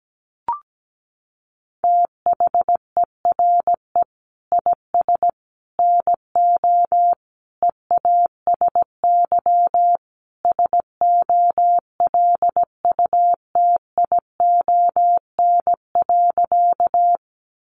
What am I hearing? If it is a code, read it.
Morse code, heard as a keyed tone, about 17 wpm, 700 Hz. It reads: THERE IS NO EASY SOLUTION.
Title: Sentences from Top 1300 Words 17wpm